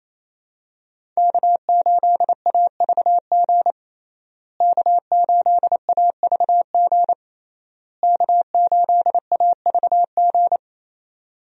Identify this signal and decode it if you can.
Morse code, heard as a keyed tone, about 28 wpm, 700 Hz. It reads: K8AVG K8AVG K8AVG
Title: USA Call Signs - 5 Characters - Repeated 3x - 28wpm